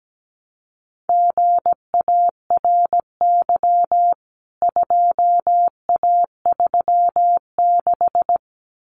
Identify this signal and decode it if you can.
Morse code, heard as a keyed tone, about 17 wpm, 700 Hz. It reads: GARY 2A36
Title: CWT Exchange 17wpm